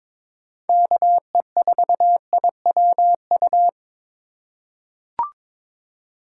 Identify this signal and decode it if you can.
Morse code, heard as a keyed tone, about 22 wpm, 700 Hz. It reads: KE4IWU
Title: USA Call Signs - 6 Characters - 4x Character Spacing - 22wpm